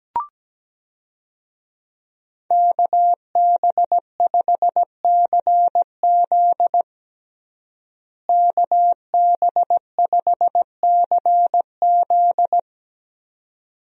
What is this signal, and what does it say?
Morse code, heard as a keyed tone, about 17 wpm, 700 Hz. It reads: KB5CZ KB5CZ
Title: USA Call Signs - 5 Characters - Repeated 2x - 17wpm